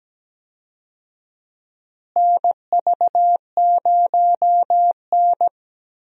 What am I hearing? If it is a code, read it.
Morse code, heard as a keyed tone, about 17 wpm, 700 Hz. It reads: NV0N
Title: USA Call Signs - 4 Characters - Repeated 3x - 17wpm